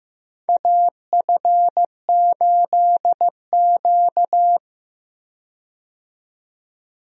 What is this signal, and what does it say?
Morse code, heard as a keyed tone, about 15 wpm, 700 Hz. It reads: AF8Q
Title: USA Call Signs - 4 Characters - 15wpm